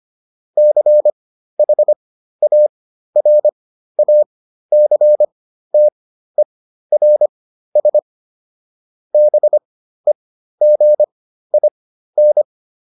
Morse code, heard as a keyed tone, about 25 wpm, spaced as for 13 wpm, 600 Hz. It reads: CHARACTERS BEGIN